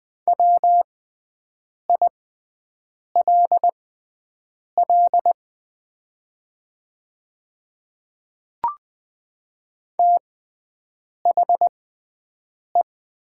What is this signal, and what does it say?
Morse code, heard as a keyed tone, about 20 wpm, spaced as for 7 wpm, 700 Hz. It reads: WILL THE
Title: 6x Character Spacing - Top 200 Words - 20wpm